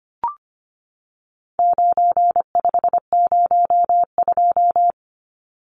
Morse code, heard as a keyed tone, about 25 wpm, 700 Hz. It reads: 9502
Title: Numbers - 4-Digit 25wpm